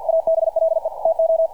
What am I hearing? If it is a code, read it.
Morse code, heard as a keyed tone, about 47 wpm, 660 Hz, with radio noise. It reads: ABLE TO
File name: archived / CRSnet